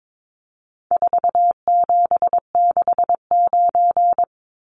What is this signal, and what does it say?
Morse code, heard as a keyed tone, about 22 wpm, 700 Hz. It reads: 4769
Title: Numbers - 4-Digit 22wpm